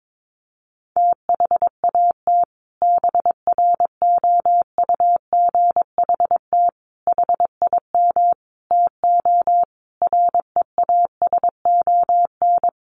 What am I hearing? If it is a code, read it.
Morse code, heard as a keyed tone, about 22 wpm, 700 Hz. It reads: THAT BROUGHT HIM TO REASON
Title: Sentences from Top 700 Words 22wpm